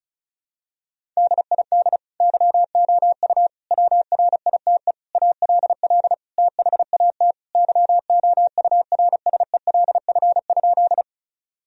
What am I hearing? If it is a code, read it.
Morse code, heard as a keyed tone, about 36 wpm, 700 Hz. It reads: DID YOU WRITE ALL THAT YOURSELF?
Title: Sentences from Top 1000 Words - Review Part 1 - 35wpm